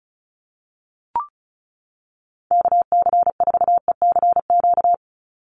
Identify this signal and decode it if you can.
Morse code, heard as a keyed tone, about 34 wpm, 700 Hz. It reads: KC4ECQ
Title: USA Call Signs 35wpm